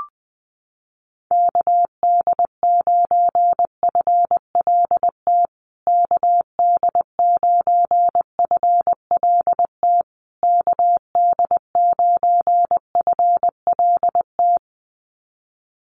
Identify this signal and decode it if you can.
Morse code, heard as a keyed tone, about 20 wpm, 700 Hz. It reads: KD9FLT KD9FLT KD9FLT
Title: USA Call Signs Repeated 3x 20wpm